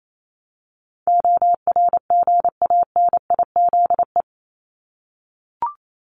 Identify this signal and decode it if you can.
Morse code, heard as a keyed tone, about 28 wpm, 700 Hz. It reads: ORGANIZE
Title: Top 1751-2000 Words 28wpm